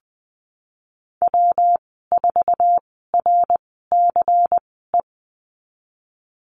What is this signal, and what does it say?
Morse code, heard as a keyed tone, about 20 wpm, spaced as for 15 wpm, 700 Hz. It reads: W4RCE
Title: 186 Lesson 30g - Letter b - Call Signs 20wpm